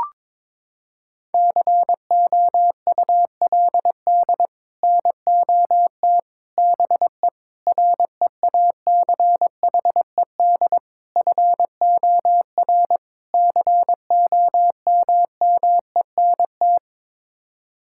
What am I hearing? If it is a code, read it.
Morse code, heard as a keyed tone, about 22 wpm, 700 Hz. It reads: COULD NOT BE REACHED FOR COMMENT
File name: practice / 22wpm / Sets of 6 Words - 22wpm